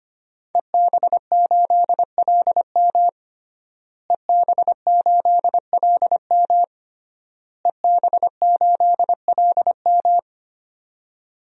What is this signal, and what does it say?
Morse code, heard as a keyed tone, about 25 wpm, 700 Hz. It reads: EB8LM EB8LM EB8LM
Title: DXCC-100 Call Signs - 5 Characters - Repeated 3x - 25wpm